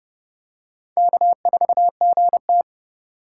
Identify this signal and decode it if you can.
Morse code, heard as a keyed tone, about 27 wpm, 700 Hz. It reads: K4GT